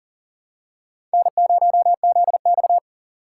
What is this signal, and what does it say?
Morse code, heard as a keyed tone, about 40 wpm, 700 Hz. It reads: N0ZX
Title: USA Call Signs - 4 Characters - 3x Character Spacing - 40wpm